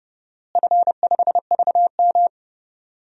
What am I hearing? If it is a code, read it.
Morse code, heard as a keyed tone, about 32 wpm, 700 Hz. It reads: F5VM
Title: DXCC-100 Call Signs - 4 Characters - Repeated 2x - 30wpm